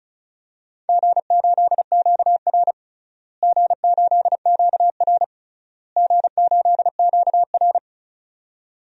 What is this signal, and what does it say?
Morse code, heard as a keyed tone, about 34 wpm, 700 Hz. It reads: G8QR G8QR G8QR
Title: DXCC-100 Call Signs - 4 Characters - Repeated 3x - 35wpm